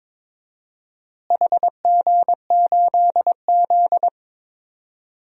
Morse code, heard as a keyed tone, about 22 wpm, 700 Hz. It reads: HG8Z
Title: DXCC-100 Call Signs - 4 Characters - Repeated 3x - 22wpm